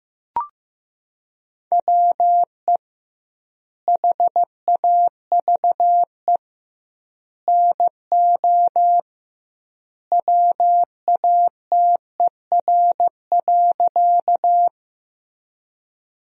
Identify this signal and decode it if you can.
Morse code, heard as a keyed tone, about 15 wpm, 700 Hz. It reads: WE HAVE NO WATER.
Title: 2x Word-Spacing - Sentences from Top 100 Words - 15wpm